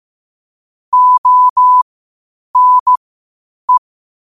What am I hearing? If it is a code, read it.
Morse code, heard as a keyed tone, about 15 wpm, spaced as for 8 wpm, 1000 Hz. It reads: ONE
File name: qotdfiles / 8wpm